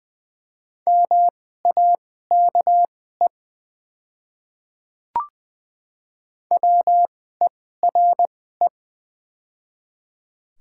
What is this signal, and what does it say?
Morse code, heard as a keyed tone, about 20 wpm, spaced as for 15 wpm, 690 Hz. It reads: MAKE WERE
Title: 2x Character Spacing - Top 100 Words - 20wpm